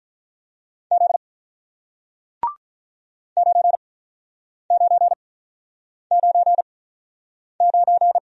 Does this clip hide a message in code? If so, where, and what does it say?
Yes: Morse code from 0.9 to 8.2 s, heard as a keyed tone, about 47 wpm, spaced as for 9 wpm, 700 Hz. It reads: Z 9999